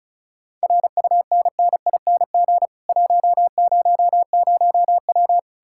Morse code, heard as a keyed tone, about 47 wpm, spaced as for 41 wpm, 700 Hz. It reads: RUNNING 100W
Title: Speed-Racing Common QSO Phrases - 35wpm